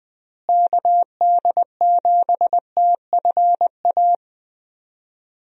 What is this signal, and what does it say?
Morse code, heard as a keyed tone, about 21 wpm, 700 Hz. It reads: KD7TFA
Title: USA Call Signs - 6 Characters - Repeated 3x - 20wpm